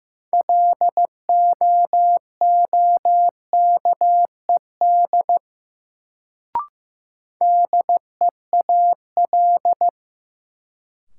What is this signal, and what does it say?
Morse code, heard as a keyed tone, about 16 wpm, 700 Hz. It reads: LOOKED DEAL
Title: Sentences from Top 800 Words - New Word Drill - 15wpm